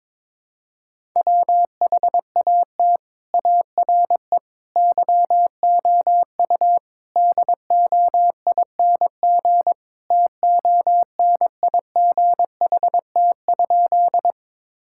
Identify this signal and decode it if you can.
Morse code, heard as a keyed tone, about 21 wpm, 700 Hz. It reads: WHAT ARE YOU DOING TONIGHT?
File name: practice / 22wpm / Sentences with 4-Word Phrases - 22wpm